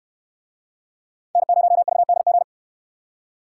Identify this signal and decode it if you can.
Morse code, heard as a keyed tone, about 68 wpm, 700 Hz. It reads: N0VDG